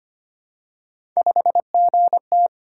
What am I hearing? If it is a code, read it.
Morse code, heard as a keyed tone, about 25 wpm, 700 Hz. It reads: 5GT
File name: Three Letter-Number Combinations - 25wpm